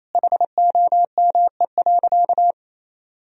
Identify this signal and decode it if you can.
Morse code, heard as a keyed tone, about 28 wpm, 700 Hz. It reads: HOME.
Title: Sentences from Top 1000 Words - Review Part 3 - 28wpm